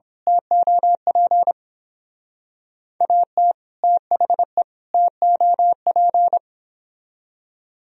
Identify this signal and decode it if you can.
Morse code, heard as a keyed tone, about 26 wpm, 700 Hz. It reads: TOP AT THE TOP
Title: Speed-Racing - Sets of 3 Words 22wpm